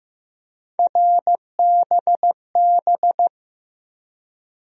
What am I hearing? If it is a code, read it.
Morse code, heard as a keyed tone, about 15 wpm, 700 Hz. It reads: RBB
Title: USA Call Sign Postfixes - 3 Letters 15wpm